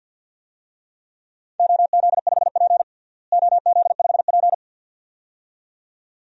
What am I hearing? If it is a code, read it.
Morse code, heard as a keyed tone, about 49 wpm, 700 Hz. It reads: OZ5P OZ5P
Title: DXCC-100 Call Signs - 4 Characters - Repeated 2x - 50wpm